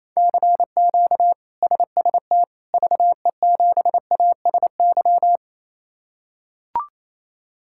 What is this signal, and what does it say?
Morse code, heard as a keyed tone, about 28 wpm, 700 Hz. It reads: CQ SST VE7ASY
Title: K1USN SST QSO Practice - 28wpm